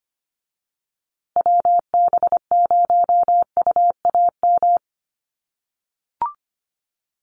Morse code, heard as a keyed tone, about 25 wpm, 700 Hz. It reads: WB0UAM
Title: USA Call Signs Repeated 2x 25wpm